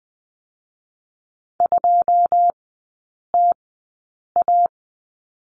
Morse code, heard as a keyed tone, about 20 wpm, spaced as for 8 wpm, 700 Hz. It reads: 2TA
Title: 107 Lesson 18e - Letter m - Three Characters 20wpm